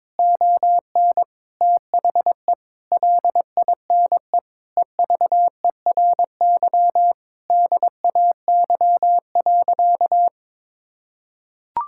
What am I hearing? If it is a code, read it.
Morse code, heard as a keyed tone, about 22 wpm, 700 Hz. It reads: ON THE LINE EVERY DAY.
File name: practice / 22wpm / Sentences with 8-Word Phrases - 22wpm